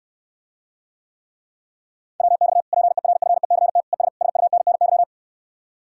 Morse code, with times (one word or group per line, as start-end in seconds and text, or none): 2.2-2.6 WX
2.7-3.8 PERFECT
3.9-4.1 ES
4.2-5.0 SUNNY